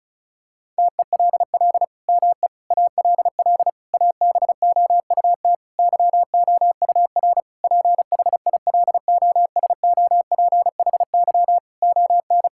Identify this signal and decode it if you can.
Morse code, heard as a keyed tone, about 34 wpm, 700 Hz. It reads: TELL ME ALL ABOUT YOUR PHILOSOPHY ON
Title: Sentences from Top 1000 Words - Review Part 6 - 35wpm